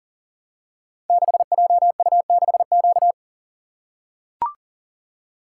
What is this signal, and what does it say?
Morse code, heard as a keyed tone, about 43 wpm, 700 Hz. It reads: 6JU6Q
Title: Code Groups - 5 Letter:Numbers - 40wpm